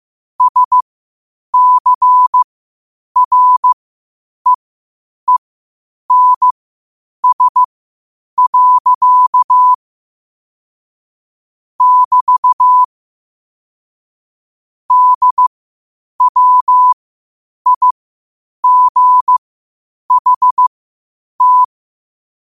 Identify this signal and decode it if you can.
Morse code, heard as a keyed tone, about 15 wpm, spaced as for 8 wpm, 1000 Hz. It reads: SCREENS. <BT> DWIGHT
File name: qotdfiles / 8wpm